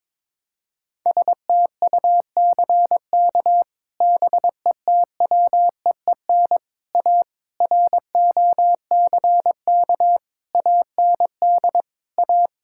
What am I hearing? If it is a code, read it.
Morse code, heard as a keyed tone, about 22 wpm, 700 Hz. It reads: STUCK BETWEEN A ROCK AND A
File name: practice / 22wpm / Sets of 8 Words - 22wpm